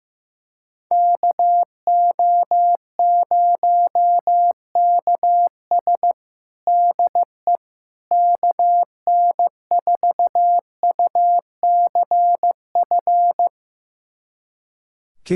Morse code, heard as a keyed tone, about 15 wpm, 700 Hz. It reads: KO0KS DE KN4UCF
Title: USA Callsign DE Callsign 15wpm